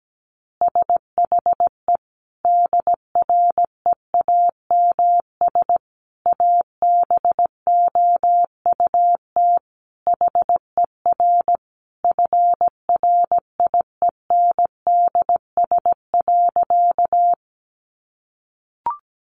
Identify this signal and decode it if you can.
Morse code, heard as a keyed tone, about 17 wpm, 700 Hz. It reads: SHE DREAMS ABOUT HER FRIENDS.
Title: Easy Phrases 17wpm